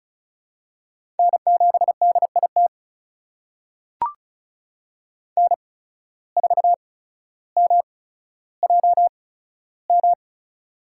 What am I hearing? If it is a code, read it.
Morse code, heard as a keyed tone, about 36 wpm, spaced as for 15 wpm, 700 Hz. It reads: N7DIT N4MJM
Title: USA Call Signs - 8x Character Spacing - 35wpm